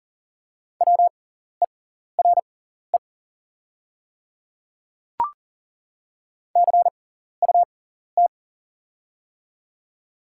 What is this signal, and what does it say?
Morse code, heard as a keyed tone, about 37 wpm, spaced as for 14 wpm, 700 Hz. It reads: WERE CUT
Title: 6x Character Spacing - Top 500 Words - 40wpm